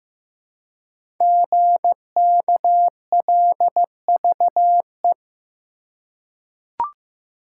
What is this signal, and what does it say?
Morse code, heard as a keyed tone, about 15 wpm, 700 Hz. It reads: GKLVE
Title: Code Groups - 5 Letter:Numbers - 15wpm